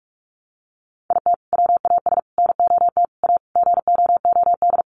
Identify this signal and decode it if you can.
Morse code, heard as a keyed tone, about 44 wpm, 700 Hz. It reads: IT WAS NOT A GOOD
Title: Sentences from Top 1100 Words 45wpm